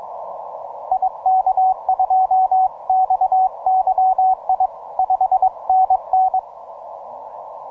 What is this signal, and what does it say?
Morse code, heard as a keyed tone, about 22 wpm, 750 Hz, with radio noise. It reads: IK2XYI 5NN